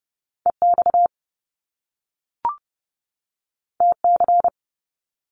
Morse code, heard as a keyed tone, about 30 wpm, 700 Hz. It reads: EX TC
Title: Top Two Letter Combinations in Words 30wpm